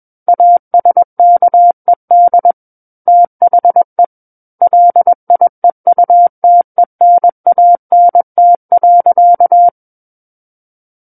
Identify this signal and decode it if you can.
Morse code, heard as a keyed tone, about 21 wpm, 700 Hz. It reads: ASKED THE LIEUTENANT.
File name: Book One / 12-Chapter 12 - 21 WPM